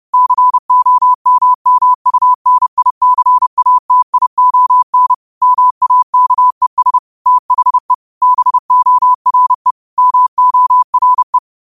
Morse code, heard as a keyed tone, about 30 wpm, 1000 Hz. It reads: COMMUNICATION MAKES THE BORE MORE
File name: qotdfiles / 30wpm